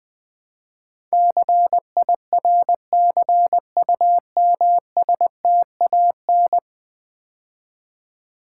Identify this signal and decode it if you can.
Morse code, heard as a keyed tone, about 20 wpm, 700 Hz. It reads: CIRCUMSTAN
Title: Build A Word - 5 Letters At A Time - 12-Letter Words - 20wpm